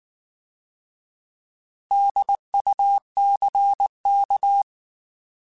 Morse code, heard as a keyed tone, about 19 wpm, 800 Hz. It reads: DUCK